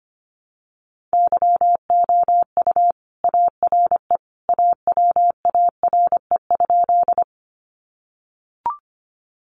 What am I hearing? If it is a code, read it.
Morse code, heard as a keyed tone, about 25 wpm, 700 Hz. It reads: YOU ARE AWARE?
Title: Sentences from Top 1200 Words 25wpm